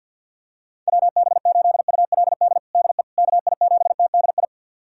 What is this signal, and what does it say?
Morse code, heard as a keyed tone, about 50 wpm, 700 Hz. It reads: WB8ULD DE KI7TDI
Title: USA Callsign DE Callsign - 6 Characters - 50wpm